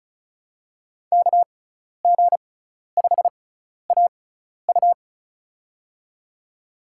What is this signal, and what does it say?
Morse code, heard as a keyed tone, about 34 wpm, spaced as for 12 wpm, 700 Hz. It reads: KG5AU
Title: USA Call Signs - 5 Characters - 6x Character Spacing - 35wpm